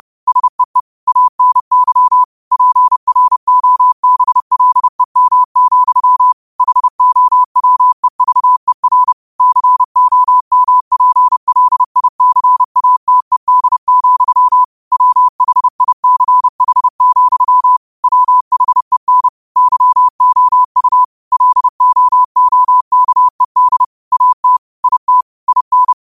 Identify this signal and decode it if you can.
Morse code, heard as a keyed tone, about 30 wpm, 1000 Hz. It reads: SEE ANY PROBLEM, HOWEVER COMPLICATED, WHICH, WHEN YOU LOOKED AT IT IN
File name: qotdfiles / 30wpm